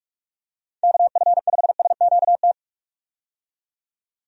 Morse code, heard as a keyed tone, about 43 wpm, 700 Hz. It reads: KF5SQT
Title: USA Call Signs - 6 Characters - 45wpm